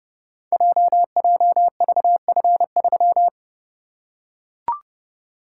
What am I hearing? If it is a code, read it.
Morse code, heard as a keyed tone, about 32 wpm, 700 Hz. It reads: JJVF3
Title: Code Groups - 5 Letter:Numbers - 30wpm